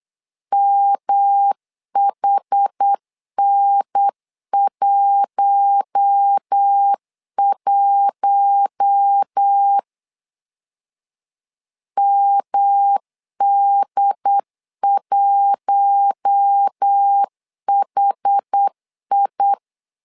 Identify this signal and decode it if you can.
Morse code, heard as a keyed tone, about 8 wpm, 800 Hz. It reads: MHN11 MD1HI
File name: LEZIONI / 4 Q1 8wpm 3035